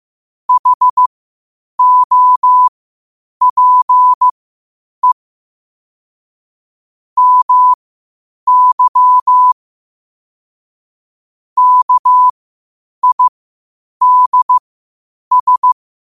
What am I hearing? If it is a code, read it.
Morse code, heard as a keyed tone, about 15 wpm, spaced as for 8 wpm, 1000 Hz. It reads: HOPE MY KIDS
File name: qotdfiles / 8wpm